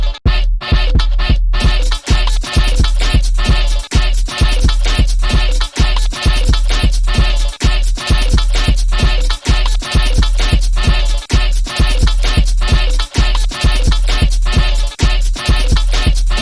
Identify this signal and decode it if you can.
There is no code.